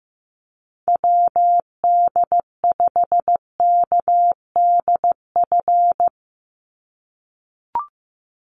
Morse code, heard as a keyed tone, about 15 wpm, 700 Hz. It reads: WD5KDF
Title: USA Call Signs 15wpm